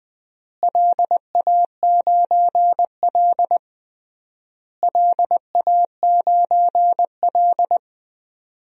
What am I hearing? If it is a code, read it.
Morse code, heard as a keyed tone, about 20 wpm, 690 Hz. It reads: LA9L LA9L
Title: DXCC-100 Call Signs - 4 Characters - Repeated 3x - 20wpm